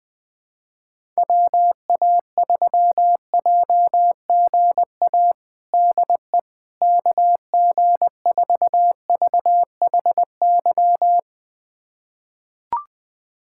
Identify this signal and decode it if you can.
Morse code, heard as a keyed tone, about 20 wpm, 700 Hz. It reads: WA3JGA DE KG4VHY
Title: USA Callsign DE Callsign 20wpm